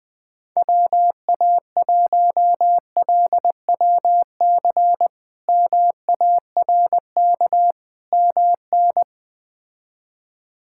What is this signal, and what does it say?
Morse code, heard as a keyed tone, about 20 wpm, 700 Hz. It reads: WA1LWC MARK MN
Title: K1USN SST QSO Practice - 20wpm